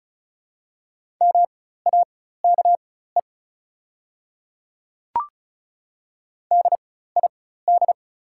Morse code, heard as a keyed tone, about 36 wpm, spaced as for 16 wpm, 690 Hz. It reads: MAKE DID
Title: 4x Character Spacing - Top 200 Words - 35wpm